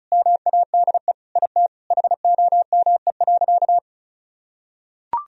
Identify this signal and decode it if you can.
Morse code, heard as a keyed tone, about 36 wpm, 700 Hz. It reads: MADE IT HOME.
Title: Sentences from Top 1000 Words - Review Part 2 - 35wpm